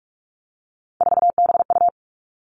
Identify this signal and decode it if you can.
Morse code, heard as a keyed tone, about 46 wpm, 700 Hz. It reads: 4BU
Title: Three Challenging Letter-Number Combinations 45wpm